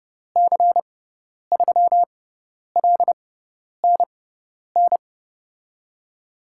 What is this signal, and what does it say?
Morse code, heard as a keyed tone, about 30 wpm, spaced as for 10 wpm, 700 Hz. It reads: C3LNN
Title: USA Call Signs - 6 Characters - 6x Character Spacing - 30wpm